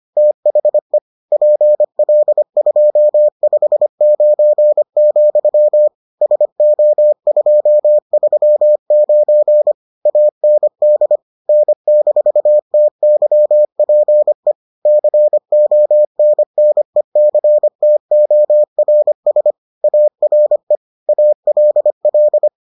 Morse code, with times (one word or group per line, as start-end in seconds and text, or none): 0.2-1.0 THE
1.3-5.9 PL259,
6.2-9.7 SO239
10.1-11.2 AND
11.5-14.5 N-TYPE
14.9-19.5 CONNECTORS
19.8-20.8 ARE
21.1-22.5 ALL